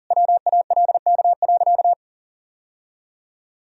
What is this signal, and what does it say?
Morse code, heard as a keyed tone, about 40 wpm, 700 Hz. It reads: WALK.
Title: Sentences from Top 1000 Words - Review Part 2 - 40wpm